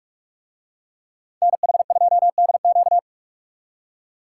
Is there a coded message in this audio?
Yes: Morse code from 1.4 to 3.0 s, heard as a keyed tone, about 42 wpm, 700 Hz. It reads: NH2DQ